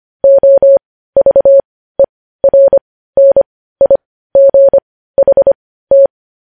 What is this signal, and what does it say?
Morse code, heard as a keyed tone, about 25 wpm, spaced as for 15 wpm, 550 Hz. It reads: OVERNIGHT
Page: news-headlines-in-morse-code-at-15-wpm